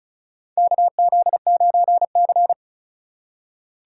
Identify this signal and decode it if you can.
Morse code, heard as a keyed tone, about 34 wpm, 700 Hz. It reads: KZ9C
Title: USA Call Signs - 4 Characters - 2x Character Spacing - 35wpm